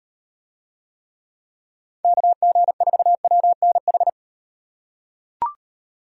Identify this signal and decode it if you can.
Morse code, heard as a keyed tone, about 38 wpm, 700 Hz. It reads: KG4WNH